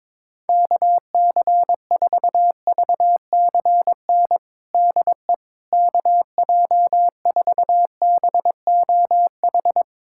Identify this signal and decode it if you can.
Morse code, heard as a keyed tone, about 22 wpm, 700 Hz. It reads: KC4VCN DE KJ4BOH
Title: USA Callsign DE Callsign - 6 Characters - 22wpm